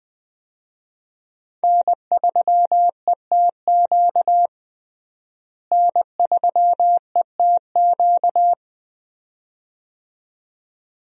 Morse code, heard as a keyed tone, about 20 wpm, 700 Hz. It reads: N3ETQ N3ETQ
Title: USA Call Signs - 5 Characters - Repeated 2x - 20wpm